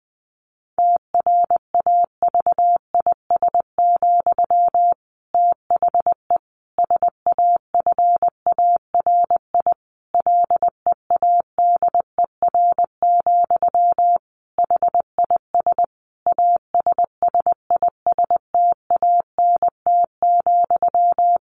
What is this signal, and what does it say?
Morse code, heard as a keyed tone, about 19 wpm, 700 Hz. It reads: TRAVIS, THE SAFARI LEADER, HIS ASSISTANT,
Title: A Sound Of Thunder 20wpm